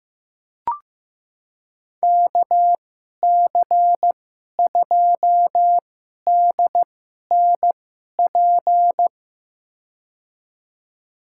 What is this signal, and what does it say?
Morse code, heard as a keyed tone, about 15 wpm, spaced as for 11 wpm, 700 Hz. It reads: KC2DNP
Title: USA Call Signs - 2x Character Spacing - 15wpm